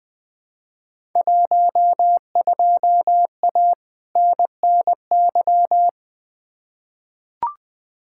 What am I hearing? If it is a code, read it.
Morse code, heard as a keyed tone, about 20 wpm, 700 Hz. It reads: 12A NNY
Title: Field Day Exchanges Encore - 20wpm